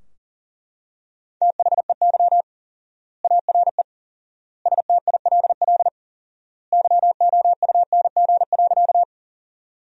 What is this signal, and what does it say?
Morse code, heard as a keyed tone, about 40 wpm, 700 Hz. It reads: THEY ARE STILL YOUNG.
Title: 4x Word-Spacing - Sentences from Top 400 Words - 40wpm